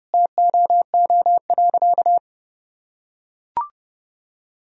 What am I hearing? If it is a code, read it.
Morse code, heard as a keyed tone, about 32 wpm, 700 Hz. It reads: TOO.
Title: Sentences from Top 1000 Words - Review Part 3 - 30wpm